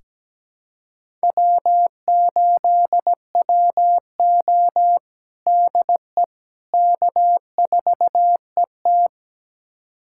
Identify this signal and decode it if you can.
Morse code, heard as a keyed tone, about 17 wpm, 700 Hz. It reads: W8WO DE K4ET
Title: USA Callsign DE Callsign - 4 Characters - 17wpm